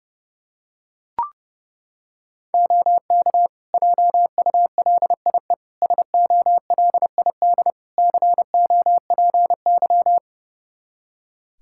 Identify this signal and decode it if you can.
Morse code, heard as a keyed tone, about 30 wpm, 700 Hz. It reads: OK JULIE SOLID COPY
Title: Practice QSO - Exchange 3 - More Info - 30wpm